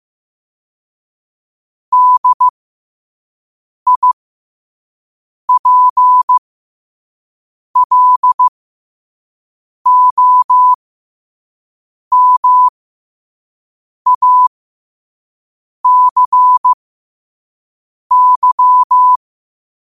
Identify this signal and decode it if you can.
Morse code, heard as a keyed tone, about 15 wpm, spaced as for 5 wpm, 1000 Hz. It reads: DIPLOMACY
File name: qotdfiles / 5wpm